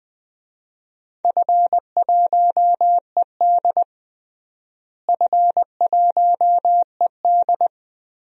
Morse code, heard as a keyed tone, about 21 wpm, 700 Hz. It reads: F1ED F1ED
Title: DXCC-100 Call Signs - 4 Characters - Repeated 3x - 20wpm